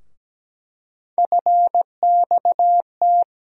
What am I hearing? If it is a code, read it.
Morse code, heard as a keyed tone, about 17 wpm, 700 Hz. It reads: FXT
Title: USA Call Sign Postfixes - 3 Letters 17wpm